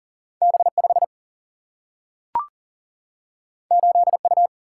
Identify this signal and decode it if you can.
Morse code, heard as a keyed tone, about 40 wpm, 700 Hz. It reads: B5 8U
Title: Single-Letter Single-Number 40wpm